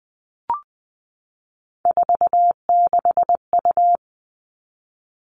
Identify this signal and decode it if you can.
Morse code, heard as a keyed tone, about 20 wpm, 700 Hz. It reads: 46U
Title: All Cut Numbers - 3-Digit 20wpm